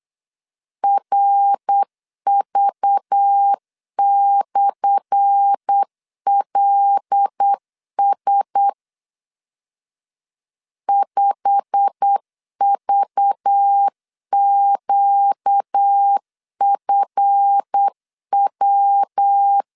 Morse code, heard as a keyed tone, about 8 wpm, 800 Hz. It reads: RV/LS 5VQFW
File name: LEZIONI / 10 SIGNS / 10C DOT 8wpm 3649